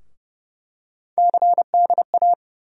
Morse code, heard as a keyed tone, about 30 wpm, 700 Hz. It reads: CDA
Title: USA Call Sign Postfixes - 3 Letters 30wpm